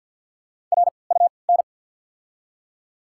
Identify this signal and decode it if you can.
Morse code, heard as a keyed tone, about 48 wpm, spaced as for 28 wpm, 700 Hz. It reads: RUN
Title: 3x Character Spacing - Top 300 Words - 50wpm